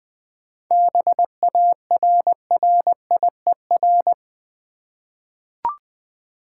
Morse code, heard as a keyed tone, about 20 wpm, 700 Hz. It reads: BARRIER
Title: 7 Letter Words - First 3 Letters Given - 20wpm